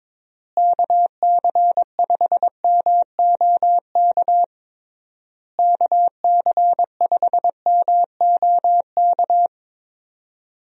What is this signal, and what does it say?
Morse code, heard as a keyed tone, about 22 wpm, 700 Hz. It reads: KC5MOK KC5MOK
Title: USA Call Signs - 6 Characters - Repeated 3x - 22wpm